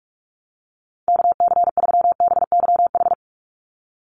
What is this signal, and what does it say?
Morse code, heard as a keyed tone, about 46 wpm, 690 Hz. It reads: KC3BYH
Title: USA Call Signs 45wpm